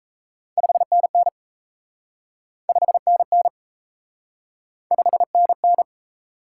Morse code, heard as a keyed tone, about 38 wpm, 700 Hz. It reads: 5NN 5NN 5NN